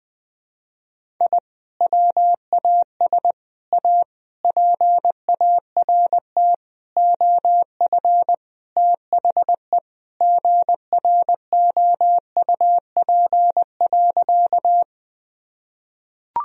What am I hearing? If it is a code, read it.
Morse code, heard as a keyed tone, about 20 wpm, 700 Hz. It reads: I WAS A PART OF THE GROUP.
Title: Sentences from Top 1000 Words - Review Part 1 - 20wpm